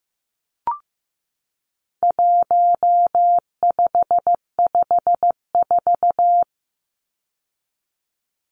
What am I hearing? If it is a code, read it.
Morse code, heard as a keyed tone, about 15 wpm, 700 Hz. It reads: 1554
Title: Numbers - 4-Digit 15wpm